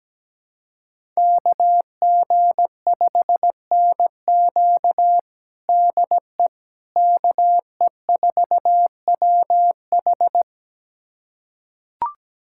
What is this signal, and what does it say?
Morse code, heard as a keyed tone, about 17 wpm, 700 Hz. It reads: KG5NQ DE KE4WH
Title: USA Callsign DE Callsign 17wpm